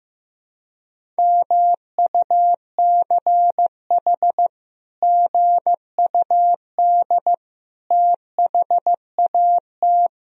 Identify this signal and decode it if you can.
Morse code, heard as a keyed tone, about 15 wpm, 700 Hz. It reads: MUCH GUD THAT
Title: Sentences from Top 200 Words and Abbreviations 15wpm